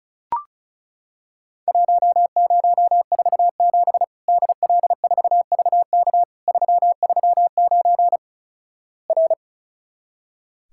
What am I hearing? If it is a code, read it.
Morse code, heard as a keyed tone, about 36 wpm, 700 Hz. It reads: 1047 DL4VK 339 R
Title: Worked All Europe DX Contest - QTC Exchange - 35wpm